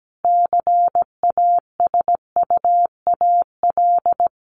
Morse code, heard as a keyed tone, about 17 wpm, 700 Hz. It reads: CASUAL
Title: Additional Easy Phrases 17wpm